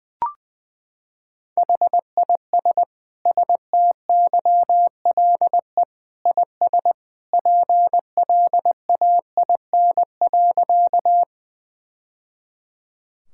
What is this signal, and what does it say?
Morse code, heard as a keyed tone, about 21 wpm, 700 Hz. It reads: HIS STYLE IS PLAIN.
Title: Sentences from Top 1000 Words - Review Part 5 - 20wpm